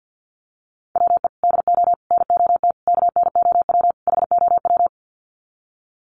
Morse code, heard as a keyed tone, about 50 wpm, 700 Hz. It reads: WE DO NOT KNOW HOW
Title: Sets of 5 Words Encore 50wpm